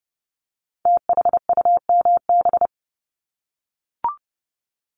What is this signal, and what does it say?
Morse code, heard as a keyed tone, about 30 wpm, 700 Hz. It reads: THUMB
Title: Multi-letter Phonemes Repeated 3x 30wpm